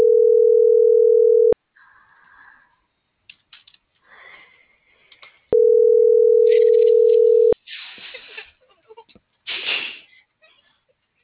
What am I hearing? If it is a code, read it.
Ringback tone (North America), 440 + 480 Hz.